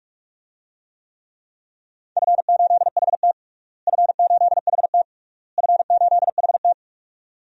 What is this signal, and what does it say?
Morse code, heard as a keyed tone, about 41 wpm, 700 Hz. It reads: F8HT F8HT F8HT
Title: DXCC-100 Call Signs - 4 Characters - Repeated 3x - 45wpm